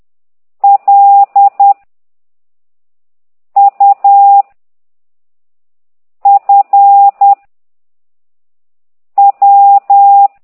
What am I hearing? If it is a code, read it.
Morse code, heard as a keyed tone, about 10 wpm, 800 Hz. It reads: L U F W
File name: ct2idw / exerc9